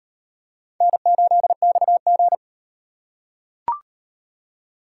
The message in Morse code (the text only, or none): N8XG